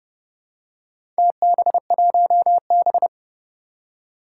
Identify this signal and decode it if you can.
Morse code, heard as a keyed tone, about 30 wpm, 690 Hz. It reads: TB1B
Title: Four Challenging Letter-Number Combinations 30wpm